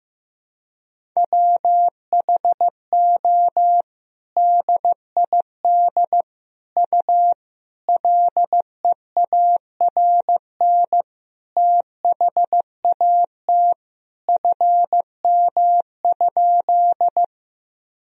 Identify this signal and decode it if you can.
Morse code, heard as a keyed tone, about 15 wpm, 700 Hz. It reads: WHO DID U LEARN THAT FM?
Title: Sentences from Top 300 Words and Abbreviations 15wpm